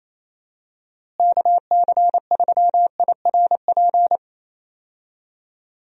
Morse code, heard as a keyed tone, about 27 wpm, 700 Hz. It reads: KC3IRP
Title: USA Call Signs - 6 Characters - Repeated 3x - 28wpm